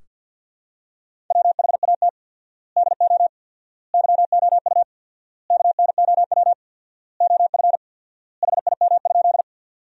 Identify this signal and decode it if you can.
Morse code, heard as a keyed tone, about 46 wpm, 700 Hz. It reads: WHAT DO YOU KNOW OF HIM?
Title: 4x Word-Spacing - Sentences from Top 100 Words - 50wpm